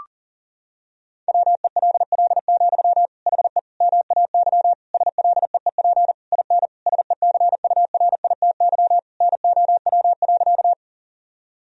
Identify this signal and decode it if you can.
Morse code, heard as a keyed tone, about 40 wpm, 700 Hz. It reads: WELL, HE MAY SLEEP IN SECURITY NOW.
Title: Sentences from Top 1000 Words - Review Part 4 - 40wpm